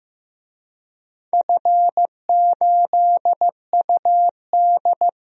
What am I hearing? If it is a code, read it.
Morse code, heard as a keyed tone, about 15 wpm, 700 Hz. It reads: F8UD
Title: DXCC-100 Call Signs - 4 Characters - 15wpm